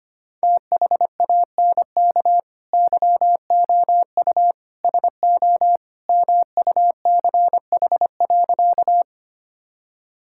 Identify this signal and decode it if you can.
Morse code, heard as a keyed tone, about 25 wpm, 700 Hz. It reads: THANK YOU SO MUCH.
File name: practice / 25wpm / Sentences with 2-Word Phrases - 25wpm